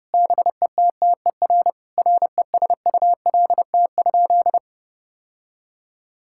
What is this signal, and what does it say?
Morse code, heard as a keyed tone, about 32 wpm, 700 Hz. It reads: BETTER RESULT?
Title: Sentences from Top 1000 Words - Review Part 6 - 30wpm